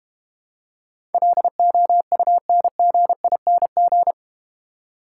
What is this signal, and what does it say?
Morse code, heard as a keyed tone, about 31 wpm, 700 Hz. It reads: LOUNGING
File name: Easy Words - Speed-Racing with Hints - 25wpm